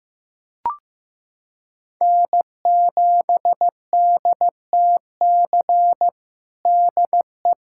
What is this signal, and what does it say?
Morse code, heard as a keyed tone, about 15 wpm, 700 Hz. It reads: N7DTC DE
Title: USA Callsign DE Callsign - 5 Characters - 15wpm